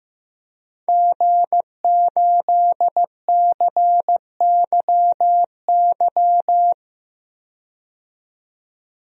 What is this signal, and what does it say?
Morse code, heard as a keyed tone, about 15 wpm, 700 Hz. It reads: G8CYY
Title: DXCC-100 Call Signs - 5 Characters - 15wpm